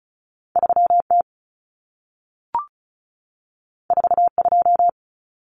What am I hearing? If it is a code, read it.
Morse code, heard as a keyed tone, about 36 wpm, 700 Hz. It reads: 3T 42
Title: Common Cut Numbers - 2-Digit 35wpm